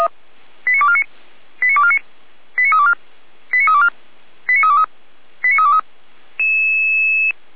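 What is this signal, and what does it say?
Five-tone selective call (ZVEI), heard as quick runs of five tones, tones of about 70 ms. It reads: 89249 89249 89324 89324 89323 89323
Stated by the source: ZVEI five-tone selective call 89570, received with an SDR.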